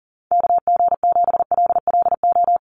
Morse code, heard as a keyed tone, about 40 wpm, 700 Hz. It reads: KG7LLO
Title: USA Call Signs 40wpm